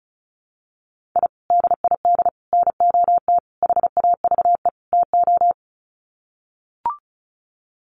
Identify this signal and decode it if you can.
Morse code, heard as a keyed tone, about 34 wpm, 700 Hz. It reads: I DID NOT HAVE TO
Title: Sets of 5 Words 35wpm